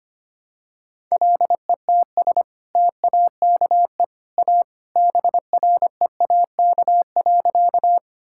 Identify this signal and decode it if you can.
Morse code, heard as a keyed tone, about 25 wpm, 700 Hz. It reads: LETS TAKE A BREAK.
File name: practice / 25wpm / Sentences with 2-Word Phrases - 25wpm